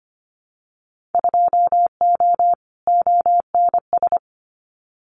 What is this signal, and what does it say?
Morse code, heard as a keyed tone, about 25 wpm, 700 Hz. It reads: 2O ONS
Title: Winter Field Day Exchanges Encore 25wpm